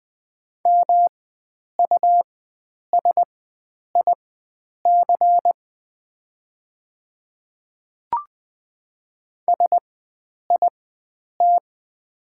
Morse code, heard as a keyed tone, about 20 wpm, spaced as for 9 wpm, 700 Hz. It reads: MUSIC SIT
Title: 4x Character Spacing - Top 500 Words - 20wpm